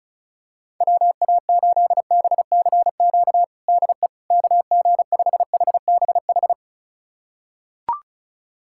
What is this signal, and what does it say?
Morse code, heard as a keyed tone, about 34 wpm, 700 Hz. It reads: WA8BCQ DE KG5HBH
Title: USA Callsign DE Callsign - 6 Characters - 35wpm